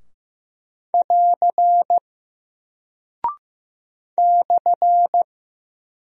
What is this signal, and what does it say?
Morse code, heard as a keyed tone, about 15 wpm, 700 Hz. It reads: <AR> /
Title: Single Character (Letter:Number:Punctuation:Prosign) 15wpm